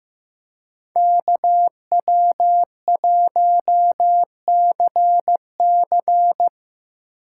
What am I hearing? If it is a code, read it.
Morse code, heard as a keyed tone, about 18 wpm, 700 Hz. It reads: KW1CC
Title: Speed Racing - USA 2x2 Call Signs 15wpm